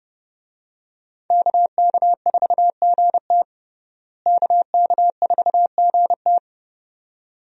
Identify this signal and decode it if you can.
Morse code, heard as a keyed tone, about 30 wpm, 700 Hz. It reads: KK4GT KK4GT
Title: USA Call Signs - 5 Characters - Repeated 2x - 30wpm